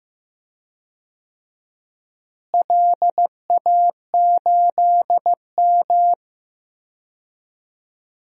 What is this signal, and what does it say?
Morse code, heard as a keyed tone, about 15 wpm, 700 Hz. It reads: LA8M
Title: DXCC-100 Call Signs - 4 Characters - Repeated 2x - 15wpm